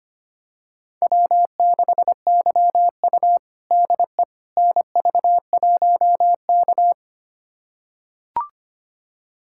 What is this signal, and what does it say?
Morse code, heard as a keyed tone, about 25 wpm, 700 Hz. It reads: W6YU DE NV1K
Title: USA Callsign DE Callsign - 4 Characters - 25wpm